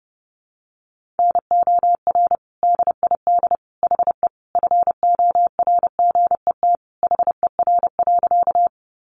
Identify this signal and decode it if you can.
Morse code, heard as a keyed tone, about 30 wpm, 700 Hz. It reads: NOR DID HE FORGET HER.
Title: Sentences from Top 1500 Words 30wpm